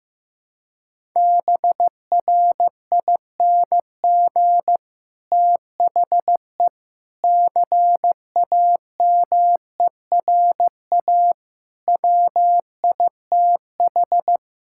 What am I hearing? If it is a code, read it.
Morse code, heard as a keyed tone, about 14 wpm, 700 Hz. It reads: BRING THE CAMERA WITH
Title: Sentences from Top 1000 Words - Review Part 5 - 15wpm